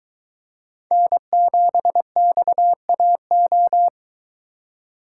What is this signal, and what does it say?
Morse code, heard as a keyed tone, about 23 wpm, 700 Hz. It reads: N7XAO